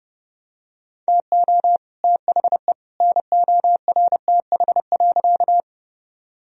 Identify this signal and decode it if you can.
Morse code, heard as a keyed tone, about 30 wpm, 700 Hz. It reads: TO THE NORTH.
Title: Sentences from Top 1000 Words - Review Part 1 - 30wpm